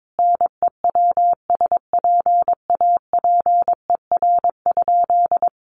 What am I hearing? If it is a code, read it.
Morse code, heard as a keyed tone, about 22 wpm, 700 Hz. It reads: NEWSPAPER?
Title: Sentences from Top 1100 Words 22wpm